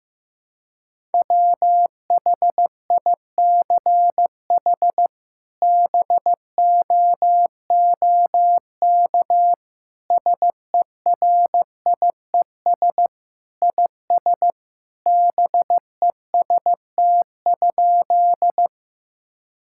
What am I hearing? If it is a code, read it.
Morse code, heard as a keyed tone, about 15 wpm, 700 Hz. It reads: WHICH BOOK SERIES IS BEST?
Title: Sentences from Top 1000 Words - Review Part 5 - 15wpm